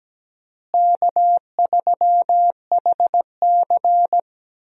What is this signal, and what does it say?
Morse code, heard as a keyed tone, about 17 wpm, 700 Hz. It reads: K3HC
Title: USA Call Signs - 4 Characters - Repeated 3x - 17wpm